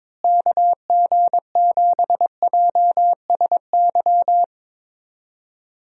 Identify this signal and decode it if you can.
Morse code, heard as a keyed tone, about 22 wpm, 700 Hz. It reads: KG7JSY
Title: USA Call Signs - 6 Characters - Repeated 2x - 22wpm